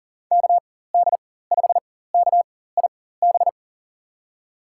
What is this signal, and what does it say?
Morse code, heard as a keyed tone, about 38 wpm, spaced as for 18 wpm, 700 Hz. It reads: KD5KIB
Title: USA Call Signs - 6 Characters - 4x Character Spacing - 40wpm